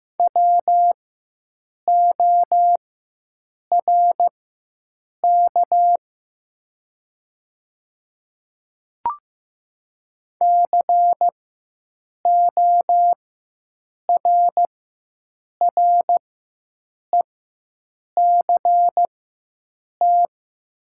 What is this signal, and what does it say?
Morse code, heard as a keyed tone, about 15 wpm, spaced as for 7 wpm, 690 Hz. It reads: WORK CORRECT
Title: 4x Character Spacing - Top 500 Words - 15wpm